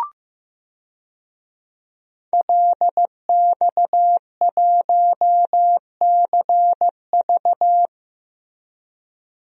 Morse code, heard as a keyed tone, about 15 wpm, 700 Hz. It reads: LX1CV
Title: DXCC-100 Call Signs - 5 Characters - 15wpm